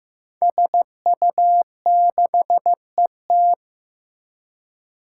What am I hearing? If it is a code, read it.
Morse code, heard as a keyed tone, about 15 wpm, 700 Hz. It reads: SU6ET
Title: Five Challenging Letter-Number Combinations 15wpm